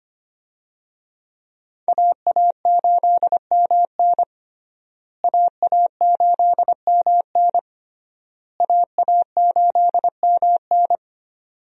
Morse code, heard as a keyed tone, about 25 wpm, 700 Hz. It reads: AA8MN AA8MN AA8MN
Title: USA Call Signs - 5 Characters - Repeated 3x - 25wpm